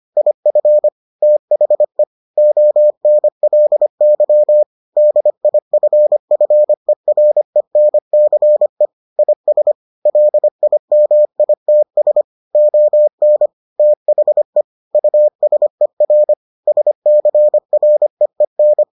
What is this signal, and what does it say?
Morse code, heard as a keyed tone, about 25 wpm, 600 Hz. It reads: IF THE ONLY DIFFERENCE IS LIMITS ON THE USER SCREEN